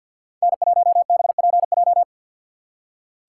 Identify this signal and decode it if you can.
Morse code, heard as a keyed tone, about 47 wpm, 700 Hz. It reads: N1BPJ
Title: USA Call Signs - 2x Character Spacing - 50wpm